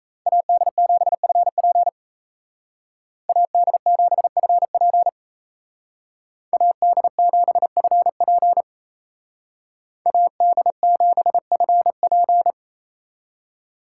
Morse code, heard as a keyed tone, about 36 wpm, 700 Hz. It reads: AD7FP AD7FP AD7FP AD7FP